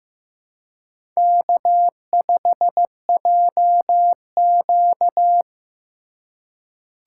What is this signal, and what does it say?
Morse code, heard as a keyed tone, about 15 wpm, 700 Hz. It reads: K5JQ
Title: USA Call Signs - 4 Characters - 2x Character Spacing - 15wpm